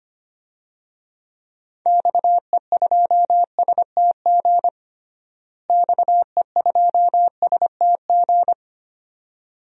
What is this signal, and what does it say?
Morse code, heard as a keyed tone, about 25 wpm, 700 Hz. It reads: XE2STG XE2STG
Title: DXCC-100 Call Signs - 6 Characters - Repeated 2x - 25wpm